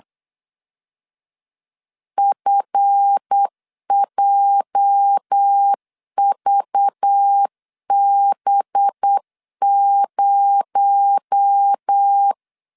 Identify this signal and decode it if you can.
Morse code, heard as a keyed tone, about 8 wpm, 800 Hz. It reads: FJVB0
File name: LEZIONI / 8 YJ45 / 8 YJ 8wpm 3313